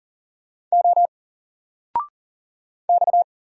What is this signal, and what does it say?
Morse code, heard as a keyed tone, about 40 wpm, 700 Hz. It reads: O X
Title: Single Letter-Number 40wpm